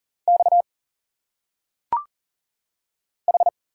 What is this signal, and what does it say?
Morse code, heard as a keyed tone, about 39 wpm, 700 Hz. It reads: X H
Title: Single Letters - Phonetic Pronunciation 40wpm